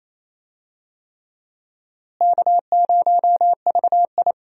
Morse code, heard as a keyed tone, about 28 wpm, 700 Hz. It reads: K0VI
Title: USA Call Signs - 4 Characters - Repeated 3x - 28wpm